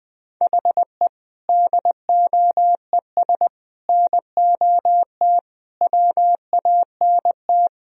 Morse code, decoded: HE DOES NOT WANT

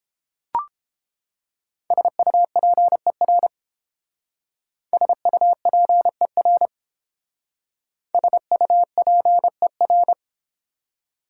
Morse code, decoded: SUPER SUPER SUPER

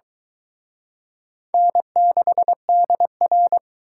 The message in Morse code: N6DR

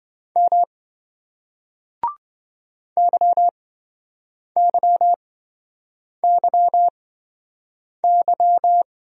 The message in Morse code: M YYYY

